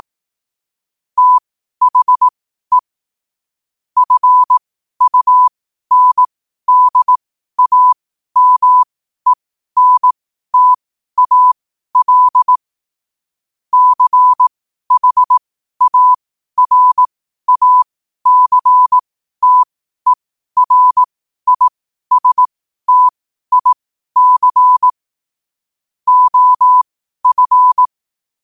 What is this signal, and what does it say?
Morse code, heard as a keyed tone, about 18 wpm, spaced as for 12 wpm, 1000 Hz. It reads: THE FUNDAMENTAL CHARACTERISTIC OF
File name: qotdfiles / 12wpm